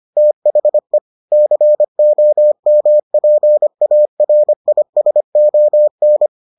Morse code, heard as a keyed tone, about 25 wpm, 600 Hz. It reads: THE COMPARISON